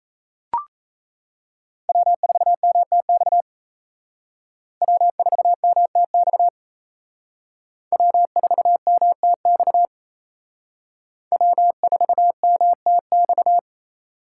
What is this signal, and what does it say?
Morse code, heard as a keyed tone, about 36 wpm, 700 Hz. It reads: W4MTX W4MTX W4MTX W4MTX